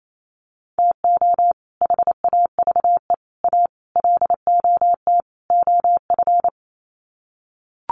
Morse code, heard as a keyed tone, about 28 wpm, 700 Hz. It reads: TO HAVE A LOT OF
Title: Sets of 5 Words Encore 28wpm